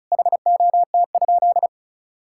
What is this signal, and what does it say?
Morse code, heard as a keyed tone, about 36 wpm, 700 Hz. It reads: HOT?
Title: Sentences from Top 1000 Words - Review Part 6 - 35wpm